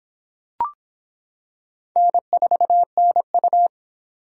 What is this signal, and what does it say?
Morse code, heard as a keyed tone, about 26 wpm, 700 Hz. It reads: N4NU